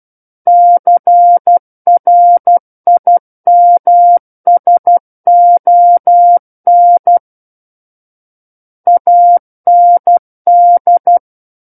Morse code, heard as a keyed tone, about 12 wpm, 700 Hz. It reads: CRIMSON AND
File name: Book One / 03-Chapter 3 - 12 WPM